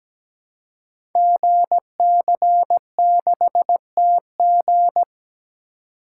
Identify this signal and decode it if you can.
Morse code, heard as a keyed tone, about 17 wpm, 700 Hz. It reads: GC6TG
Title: Code Groups - 5 Letter:Numbers - 17wpm